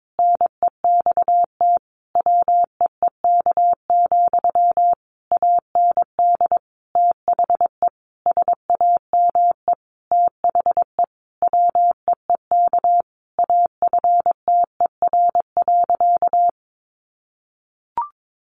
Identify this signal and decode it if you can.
Morse code, heard as a keyed tone, about 22 wpm, 700 Hz. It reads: NEXT WEEK, AND THE SAME THE WEEK AFTER.
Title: Sentences from Top 500 Words 22wpm